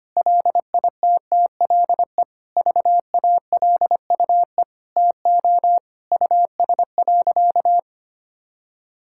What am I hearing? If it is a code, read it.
Morse code, heard as a keyed tone, about 25 wpm, 700 Hz. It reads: LITTLE VALUE TO US.
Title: Sentences from Top 1000 Words - Review Part 5 - 25wpm